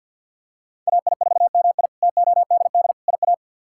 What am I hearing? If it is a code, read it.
Morse code, heard as a keyed tone, about 46 wpm, 700 Hz. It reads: AI4MI TODD IA
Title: K1USN SST QSO Practice - 50wpm